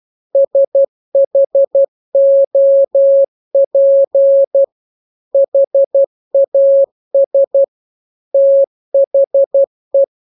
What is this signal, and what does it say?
Morse code, heard as a keyed tone, about 12 wpm, 550 Hz. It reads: SHOP HAS THE